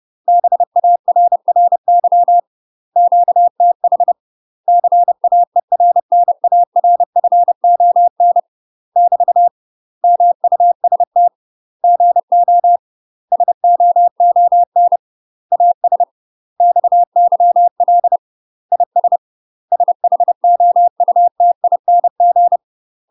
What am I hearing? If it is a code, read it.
Morse code, heard as a keyed tone, about 30 wpm, 700 Hz. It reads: BARRY QTH CAERNARFON <BT> MUST GO SOON AS XYL IS SHOUTING
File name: UK MASTER QSOs TRAINING / 30WPM / UK MASTER QSOs 0003